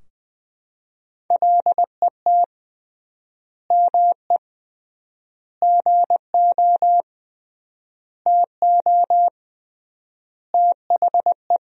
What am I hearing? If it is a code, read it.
Morse code, heard as a keyed tone, about 20 wpm, 700 Hz. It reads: LET ME GO TO THE
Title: 3x Word-Spacing - Sentences from Top 300 Words - 20wpm